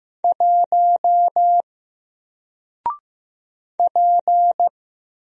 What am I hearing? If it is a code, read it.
Morse code, heard as a keyed tone, about 15 wpm, 700 Hz. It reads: 1 P
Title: Single Letter-Number Alt 15wpm